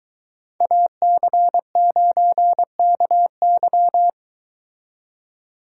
Morse code, heard as a keyed tone, about 22 wpm, 700 Hz. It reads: AC9KY